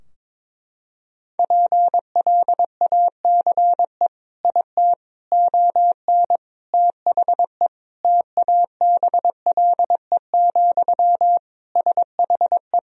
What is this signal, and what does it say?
Morse code, heard as a keyed tone, about 22 wpm, 700 Hz. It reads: PLACE IT ON THE TABLE, SHE
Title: Sentences from Top 1000 Words - Review Part 2 - 22wpm